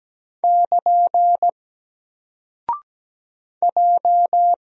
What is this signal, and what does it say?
Morse code, heard as a keyed tone, about 17 wpm, 700 Hz. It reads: <KN> J
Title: Single Character (Letter:Number:Punctuation:Prosign) 17wpm